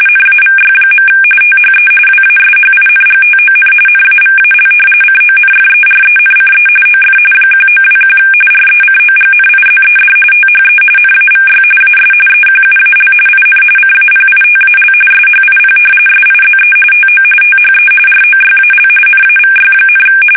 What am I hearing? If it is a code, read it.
RTTY, about 45.45 baud. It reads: NAVY. THE WASHINGTON NAVAL CONFERENCE OF 1921 RECOGNIZED THE NAVY AS EQUAL IN CAPITAL SHIP SIZE TO THE ROYAL NAVY,
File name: fsk850 / usnavy1